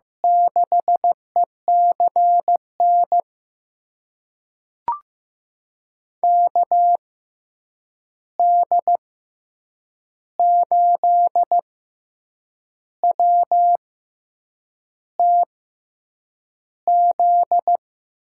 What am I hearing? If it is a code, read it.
Morse code, heard as a keyed tone, about 15 wpm, spaced as for 7 wpm, 700 Hz. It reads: 6ECN KD8WTZ
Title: USA Call Signs - 6 Characters - 6x Character Spacing - 15wpm